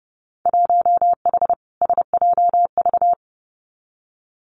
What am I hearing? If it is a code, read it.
Morse code, heard as a keyed tone, about 30 wpm, 700 Hz. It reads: 1H SJV
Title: Winter Field Day Exchanges 30wpm